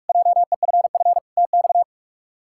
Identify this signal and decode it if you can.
Morse code, heard as a keyed tone, about 46 wpm, 700 Hz. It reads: JEFF TX
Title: K1USN SST QSO Practice - 45wpm